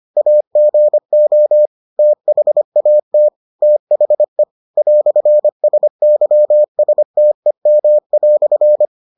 Morse code, heard as a keyed tone, about 25 wpm, 600 Hz. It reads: AGO THAT THE "SYSTEM"